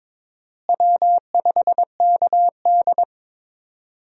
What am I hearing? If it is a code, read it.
Morse code, heard as a keyed tone, about 22 wpm, 700 Hz. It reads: W5KD
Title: USA Call Signs - 4 Characters - Repeated 3x - 22wpm